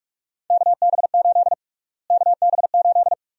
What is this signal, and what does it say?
Morse code, heard as a keyed tone, about 46 wpm, 690 Hz. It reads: KB8 KB8
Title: USA Call Sign Prefixes - Repeated 2x - 45wpm